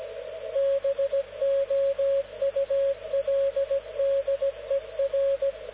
Morse code, heard as a keyed tone, about 17 wpm, 550 Hz, with radio noise. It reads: BOULDER
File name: luxorion / Radio / dk0wcy-spaceweather-10144khz